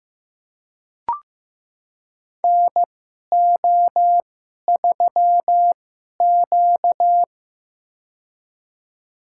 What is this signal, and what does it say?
Morse code, heard as a keyed tone, about 15 wpm, spaced as for 11 wpm, 700 Hz. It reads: NO3Q
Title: USA Call Signs - 4 Characters - 2x Character Spacing - 15wpm